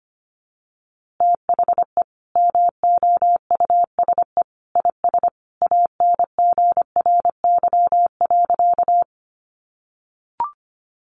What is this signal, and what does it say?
Morse code, heard as a keyed tone, about 25 wpm, 700 Hz. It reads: THE MOUSE IS ANGRY.
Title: Easy Phrases 25wpm